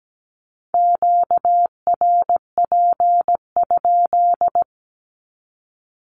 Morse code, heard as a keyed tone, about 17 wpm, 700 Hz. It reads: QRP?
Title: Q-Codes 17wpm